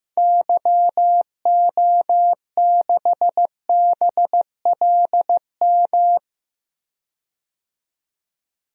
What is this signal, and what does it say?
Morse code, heard as a keyed tone, about 15 wpm, 700 Hz. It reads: YO6BLM
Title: DXCC-100 Call Signs - 6 Characters - 15wpm